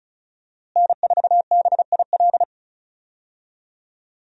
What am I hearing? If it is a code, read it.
Morse code, heard as a keyed tone, about 36 wpm, 700 Hz. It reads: N4BIL
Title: USA Call Signs - 5 Characters - Repeated 2x - 35wpm